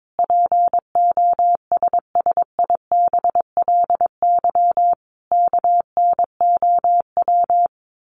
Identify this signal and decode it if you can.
Morse code, heard as a keyed tone, about 22 wpm, 700 Hz. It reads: POSSIBLY KNOW
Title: Sentences from Top 1300 Words 22wpm